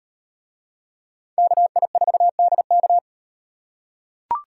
Morse code, heard as a keyed tone, about 37 wpm, 700 Hz. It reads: KI4DK